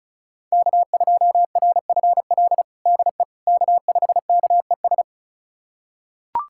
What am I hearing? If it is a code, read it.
Morse code, heard as a keyed tone, about 34 wpm, 700 Hz. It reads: K2RFL DE K5KES
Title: USA Callsign DE Callsign - 5 Characters - 35wpm